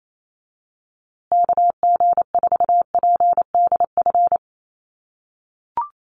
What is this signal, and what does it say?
Morse code, heard as a keyed tone, about 28 wpm, 700 Hz. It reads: KG4PDF
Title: USA Call Signs Repeated 3x 28wpm